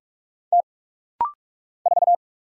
Morse code, heard as a keyed tone, about 46 wpm, 700 Hz. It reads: T 4
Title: Single Letter-Number Alt - Rapid-Fire 45wpm